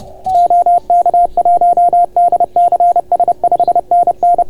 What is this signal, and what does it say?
Morse code, heard as a keyed tone, about 31 wpm, 690 Hz. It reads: OK1DCS 5NN